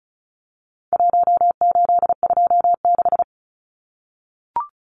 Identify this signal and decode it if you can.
Morse code, heard as a keyed tone, about 34 wpm, 700 Hz. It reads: 1826
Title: Numbers - Contest Style - 4-Digit 35wpm